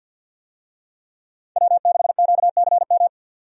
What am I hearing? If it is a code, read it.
Morse code, heard as a keyed tone, about 51 wpm, 700 Hz. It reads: W6QCM
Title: USA Call Signs - 5 Characters - 50wpm